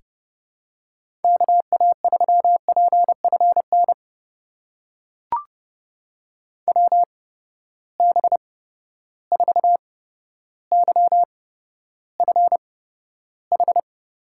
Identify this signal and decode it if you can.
Morse code, heard as a keyed tone, about 30 wpm, spaced as for 13 wpm, 700 Hz. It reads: KA3PFN WB4YFH
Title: USA Call Signs - 6 Characters - 8x Character Spacing - 30wpm